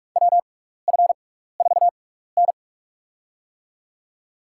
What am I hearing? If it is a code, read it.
Morse code, heard as a keyed tone, about 46 wpm, spaced as for 15 wpm, 700 Hz. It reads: WF4N
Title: USA Call Signs - 4 Characters - 6x Character Spacing - 45wpm